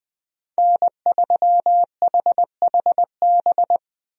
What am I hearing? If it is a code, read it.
Morse code, heard as a keyed tone, about 20 wpm, 700 Hz. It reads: N3HHB
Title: USA Call Signs - 5 Characters - Repeated 2x - 20wpm